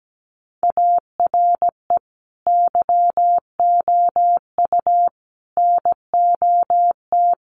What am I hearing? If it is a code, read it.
Morse code, heard as a keyed tone, about 17 wpm, 700 Hz. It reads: ARE YOU NOT
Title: Sentences from Top 1000 Words 17wpm